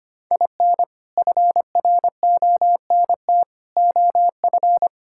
Morse code, heard as a keyed tone, about 25 wpm, 700 Hz. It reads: IN FRONT OF
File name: practice / 25wpm / Sentences with 2-Word Phrases - 25wpm